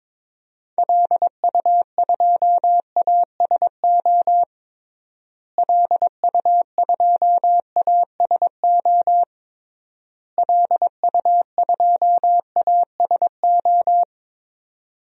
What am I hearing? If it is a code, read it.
Morse code, heard as a keyed tone, about 21 wpm, 700 Hz. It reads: LU2ASO LU2ASO LU2ASO
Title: DXCC-100 Call Signs - 6 Characters - Repeated 3x - 22wpm